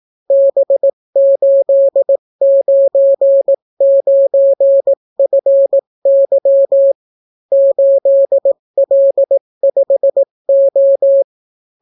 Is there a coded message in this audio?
Yes: Morse code from 0.3 to 11.2 s, heard as a keyed tone, about 18 wpm, 550 Hz. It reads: B899FY 8L5O